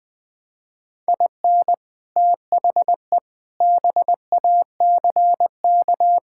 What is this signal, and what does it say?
Morse code, heard as a keyed tone, about 20 wpm, 700 Hz. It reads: IN THE BACK
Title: Sentences with 4-Word Phrases - 20wpm